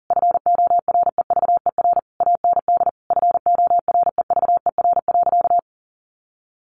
Morse code, heard as a keyed tone, about 40 wpm, 700 Hz. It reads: FOREVER AND FOREVER.
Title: A Sound Of Thunder 40wpm